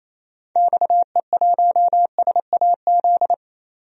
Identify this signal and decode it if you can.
Morse code, heard as a keyed tone, about 28 wpm, 700 Hz. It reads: XE1SAZ
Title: DXCC-100 Call Signs - 6 Characters - 28wpm